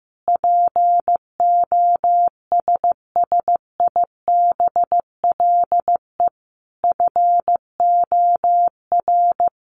Morse code, heard as a keyed tone, about 15 wpm, 700 Hz. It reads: POSSIBLE FOR
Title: Sentences from Top 900 Words 15wpm